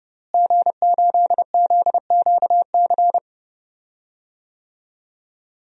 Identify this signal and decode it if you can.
Morse code, heard as a keyed tone, about 30 wpm, 700 Hz. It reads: G8ZQC
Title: DXCC-100 Call Signs - 5 Characters - Repeated 3x - 30wpm